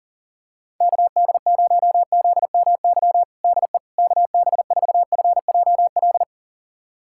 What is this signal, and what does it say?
Morse code, heard as a keyed tone, about 37 wpm, 700 Hz. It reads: KD0ZMY DE KB4FJL
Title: USA Callsign DE Callsign - 6 Characters - 40wpm